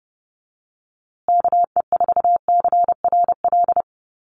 Morse code, heard as a keyed tone, about 30 wpm, 700 Hz. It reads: KE4CRL
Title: USA Call Signs Repeated 3x 30wpm